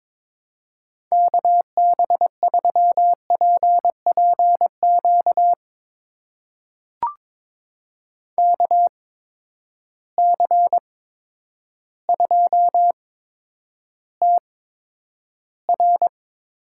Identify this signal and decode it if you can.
Morse code, heard as a keyed tone, about 21 wpm, spaced as for 10 wpm, 700 Hz. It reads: KB3PPQ KC2TR
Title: USA Call Signs - 6 Characters - 8x Character Spacing - 22wpm